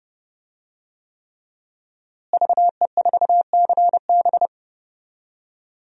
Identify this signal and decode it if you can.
Morse code, heard as a keyed tone, about 30 wpm, 700 Hz. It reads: VE4CB
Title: DXCC-100 Call Signs - 5 Characters - Repeated 3x - 30wpm